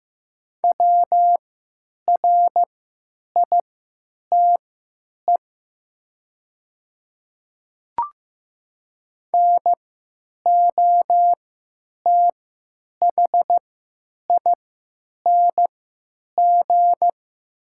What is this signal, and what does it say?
Morse code, heard as a keyed tone, about 15 wpm, spaced as for 9 wpm, 700 Hz. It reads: WRITE NOTHING
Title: 3x Character Spacing - Top 500 Words - 15wpm